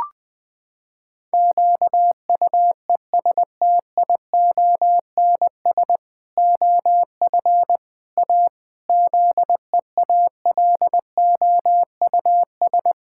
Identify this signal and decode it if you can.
Morse code, heard as a keyed tone, about 20 wpm, 700 Hz. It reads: QUESTIONS OF A ZEALOUS
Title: Pangram Challenge v2 - 20wpm